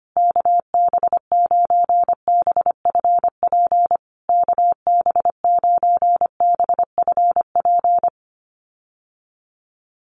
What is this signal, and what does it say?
Morse code, heard as a keyed tone, about 25 wpm, 700 Hz. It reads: KB9BFP KB9BFP
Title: USA Call Signs Repeated 3x 25wpm